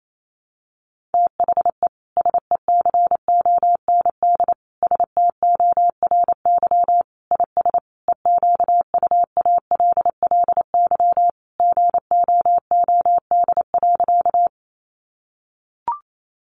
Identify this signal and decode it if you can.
Morse code, heard as a keyed tone, about 28 wpm, 700 Hz. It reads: THE SECOND STORY IS EQUALLY GOOD.
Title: Sentences from Top 1400 Words 28wpm